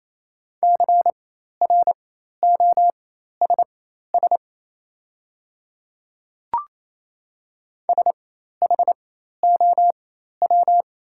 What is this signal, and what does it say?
Morse code, heard as a keyed tone, about 27 wpm, spaced as for 13 wpm, 690 Hz. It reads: CROSS SHOW
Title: 4x Character Spacing - Top 500 Words - 28wpm